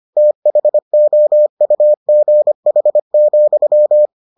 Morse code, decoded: THOUGH,

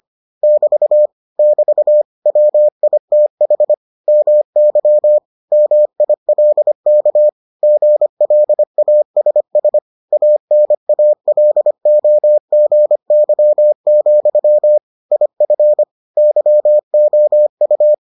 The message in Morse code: <BT> <BT> WITH MY MILK GLASS ANALOGY, IF YOU

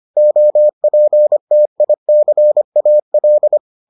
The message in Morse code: OPTICAL

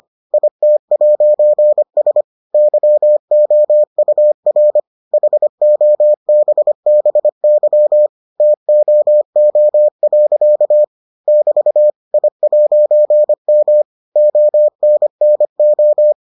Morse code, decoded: IT'S YOUR HOBBY TOO. <BT> I'M ONNO

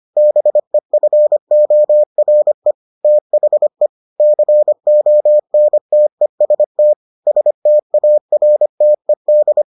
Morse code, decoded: BEFORE THE CONTEST STARTED

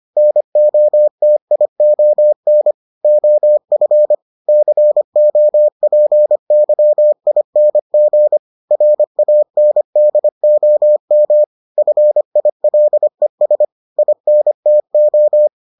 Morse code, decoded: NOTION OF COPYING RANDOM FILES INTO

